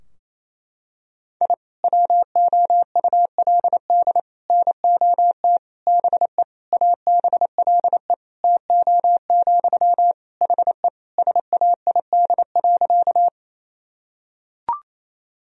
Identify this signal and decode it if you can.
Morse code, heard as a keyed tone, about 28 wpm, 700 Hz. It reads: I WOULD NOT BE ABLE TO, HE SAID.
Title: Sentences from Top 1000 Words - Review Part 2 - 28wpm